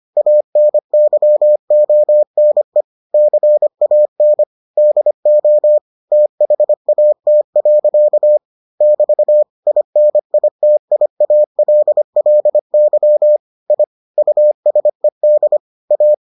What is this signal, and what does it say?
Morse code, heard as a keyed tone, about 25 wpm, 600 Hz. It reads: ANYONE CAN DO THAT. <BT> INITIALLY I USED A